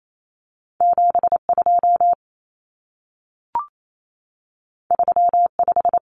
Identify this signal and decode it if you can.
Morse code, heard as a keyed tone, about 28 wpm, 700 Hz. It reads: 72 35
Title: Numbers - 2-Digit 28wpm